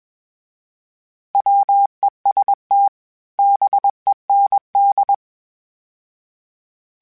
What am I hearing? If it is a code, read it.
Morse code, heard as a keyed tone, about 21 wpm, 800 Hz. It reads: WEST BEND
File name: fromSMs / FromSMs mp3s / Cities@20